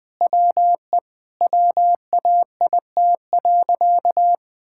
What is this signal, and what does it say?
Morse code, heard as a keyed tone, about 21 wpm, 700 Hz. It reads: WE WAIT.
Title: Sentences with 9-Word Phrases - Spoken Beginning - 20wpm